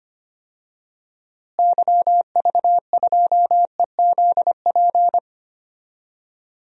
Morse code, heard as a keyed tone, about 25 wpm, 700 Hz. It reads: YV2EZP
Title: DXCC-100 Call Signs - 6 Characters - Repeated 3x - 25wpm